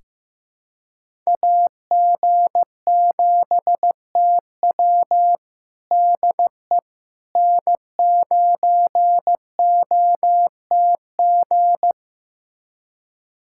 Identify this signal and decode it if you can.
Morse code, heard as a keyed tone, about 15 wpm, 700 Hz. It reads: AG7TW DE N9OTG
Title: USA Callsign DE Callsign - 5 Characters - 15wpm